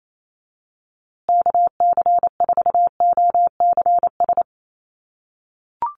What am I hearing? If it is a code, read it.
Morse code, heard as a keyed tone, about 28 wpm, 700 Hz. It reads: KC4OCS
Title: USA Call Signs Repeated 2x 28wpm